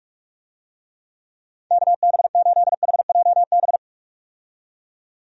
Morse code, heard as a keyed tone, about 46 wpm, 700 Hz. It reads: KB8HJB
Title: USA Call Signs - 6 Characters - 45wpm